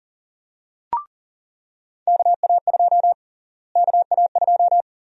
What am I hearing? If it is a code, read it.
Morse code, heard as a keyed tone, about 37 wpm, 700 Hz. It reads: KA2 KA2
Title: USA Call Sign Prefixes - Repeated 2x - 40wpm